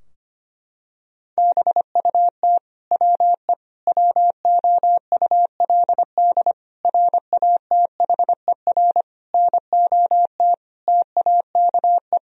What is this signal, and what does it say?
Morse code, heard as a keyed tone, about 25 wpm, 700 Hz. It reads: BUT WE WOULD RATHER NOT TAKE
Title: Sentences from Top 1000 Words - Review Part 4 - 25wpm